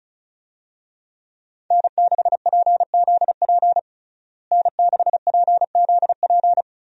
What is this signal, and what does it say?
Morse code, heard as a keyed tone, about 36 wpm, 700 Hz. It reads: N6PZP N6PZP
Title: USA Call Signs - 5 Characters - Repeated 2x - 35wpm